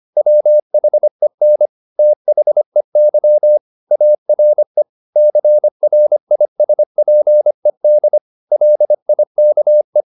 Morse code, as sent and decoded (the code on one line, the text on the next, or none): .-- .... -. / - .... -.-- / .- .-. / -.-. .-. .. ... .--. -.. / .-.. .. -.- .
WHEN THEY ARE CRISPED LIKE